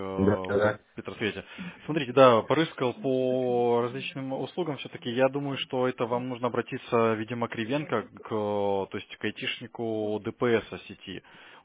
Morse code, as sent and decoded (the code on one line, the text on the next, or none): none